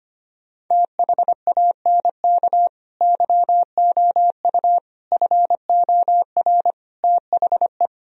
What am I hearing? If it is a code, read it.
Morse code, heard as a keyed tone, about 25 wpm, 700 Hz. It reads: THANK YOU FOR THE
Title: Sentences from Top 1000 Words - Review Part 4 - 25wpm